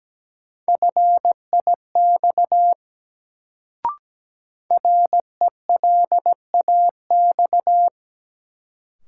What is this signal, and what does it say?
Morse code, heard as a keyed tone, about 17 wpm, 700 Hz. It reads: FIX RELAX
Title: Sentences from Top 1400 Words - New Word Drill Alt - 17wpm